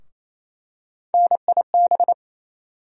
Morse code, heard as a keyed tone, about 28 wpm, 700 Hz. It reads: NIB
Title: USA Call Sign Postfixes - 3 Letters 28wpm